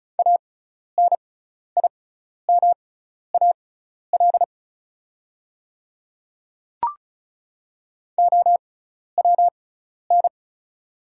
Morse code, heard as a keyed tone, about 36 wpm, spaced as for 12 wpm, 690 Hz. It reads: ANIMAL OWN
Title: 6x Character Spacing - Top 300 Words - 35wpm